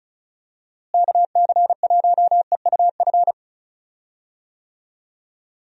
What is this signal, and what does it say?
Morse code, heard as a keyed tone, about 36 wpm, 700 Hz. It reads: KC1EUF
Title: USA Call Signs - 6 Characters - Repeated 2x - 35wpm